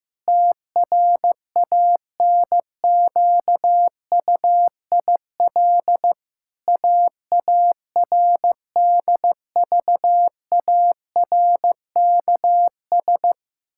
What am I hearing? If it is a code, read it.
Morse code, heard as a keyed tone, about 15 wpm, 700 Hz. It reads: TRANQUIL AARDVARKS
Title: Pangram Challenge v2 - 15wpm